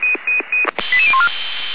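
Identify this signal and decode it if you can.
Five-tone selective call (ZVEI), heard as quick runs of five tones, tones of about 70 ms. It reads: 88014
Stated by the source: Five-tone ZVEI selective call received on SDR.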